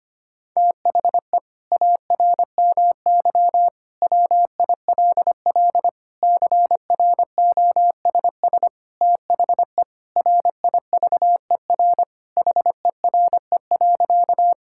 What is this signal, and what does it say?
Morse code, heard as a keyed tone, about 25 wpm, 700 Hz. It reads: THE ARMY WILL CROSS THE RIVER HERE.
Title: Sentences from Top 1000 Words - Review Part 5 - 25wpm